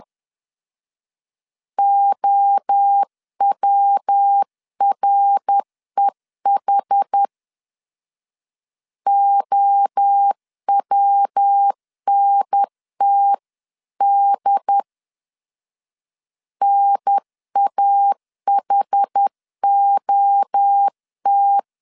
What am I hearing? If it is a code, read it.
Morse code, heard as a keyed tone, about 10 wpm, 800 Hz. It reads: OWREH OWNTD NAHOT
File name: 10wpm